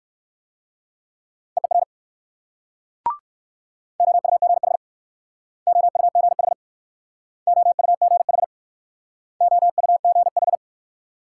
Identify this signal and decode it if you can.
Morse code, heard as a keyed tone, about 58 wpm, 690 Hz. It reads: EER OUGH OUGH OUGH OUGH